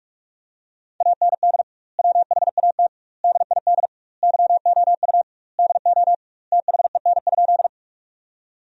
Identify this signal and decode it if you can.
Morse code, heard as a keyed tone, about 44 wpm, 700 Hz. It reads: AND WHAT DID YOU DO THEN?
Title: 2x Word-Spacing - Sentences from Top 100 Words - 45wpm